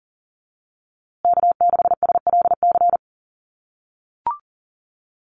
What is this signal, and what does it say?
Morse code, heard as a keyed tone, about 40 wpm, 700 Hz. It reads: K6SLC
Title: USA Call Signs Repeated 2x 40wpm